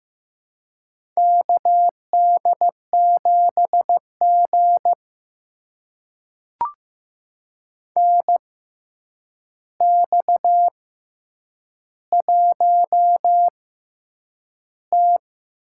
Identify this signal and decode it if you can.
Morse code, heard as a keyed tone, about 15 wpm, spaced as for 8 wpm, 700 Hz. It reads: KD7G NX1T
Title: USA Call Signs - 4 Characters - 6x Character Spacing - 15wpm